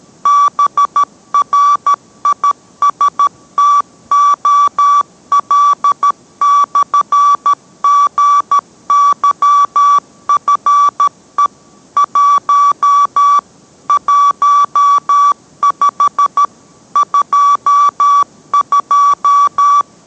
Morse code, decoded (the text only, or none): BRISTOL/GYFE 1 15 22